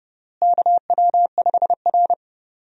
Morse code, heard as a keyed tone, about 30 wpm, 700 Hz. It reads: KW5R
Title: USA Call Signs - 4 Characters - Repeated 3x - 30wpm